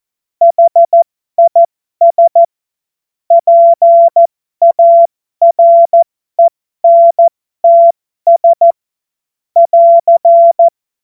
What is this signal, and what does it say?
Morse code, heard as a keyed tone, about 14 wpm, spaced as for 12 wpm, 680 Hz. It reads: HIS PARENTS <AR>
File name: Morsekurs / Witze